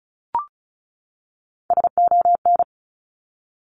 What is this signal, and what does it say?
Morse code, heard as a keyed tone, about 36 wpm, 700 Hz. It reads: SON
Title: Top Three Letter Combinations in Words 35wpm